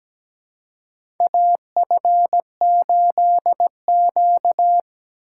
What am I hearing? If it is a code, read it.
Morse code, heard as a keyed tone, about 17 wpm, 700 Hz. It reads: AF8Q
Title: USA Call Signs - 4 Characters - 3x Character Spacing - 17wpm